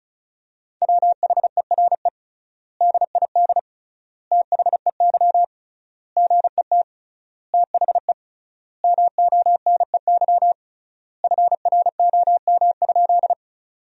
Morse code, heard as a keyed tone, about 36 wpm, 700 Hz. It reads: WHERE DID THEY GET THE MONEY FROM?
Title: 3x Word-Spacing - Sentences from Top 400 Words - 35wpm